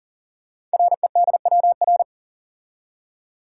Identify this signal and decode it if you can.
Morse code, heard as a keyed tone, about 39 wpm, 700 Hz. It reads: REDWR
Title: Code Groups - 5 Letters - 40wpm